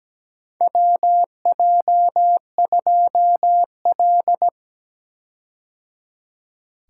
Morse code, heard as a keyed tone, about 17 wpm, 700 Hz. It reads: WJ2L
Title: K1USN SST QSO Practice - 17wpm